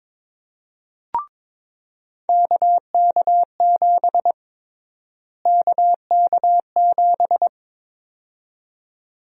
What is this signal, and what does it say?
Morse code, heard as a keyed tone, about 22 wpm, 700 Hz. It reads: KK7 KK7
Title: USA Call Sign Prefixes - Repeated 2x - 22wpm